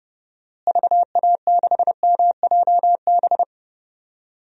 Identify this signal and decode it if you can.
Morse code, heard as a keyed tone, about 30 wpm, 700 Hz. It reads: VA6MJB
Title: DXCC-100 Call Signs - 6 Characters - Repeated 2x - 30wpm